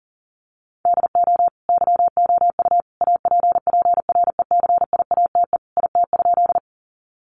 Morse code, heard as a keyed tone, about 40 wpm, 700 Hz. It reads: DO YOU APPRECIATE IT?
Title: Sentences from Top 1300 Words 40wpm